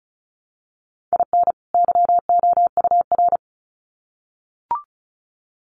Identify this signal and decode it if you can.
Morse code, heard as a keyed tone, about 34 wpm, 700 Hz. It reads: IN YOUR
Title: Sets of 2 Words 35wpm